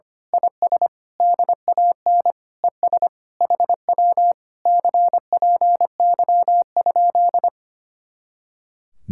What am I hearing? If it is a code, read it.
Morse code, heard as a keyed tone, about 24 wpm, 700 Hz. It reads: IS DAN ES HW CPY?
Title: Speed-Racing Common QSO Phrases - 25wpm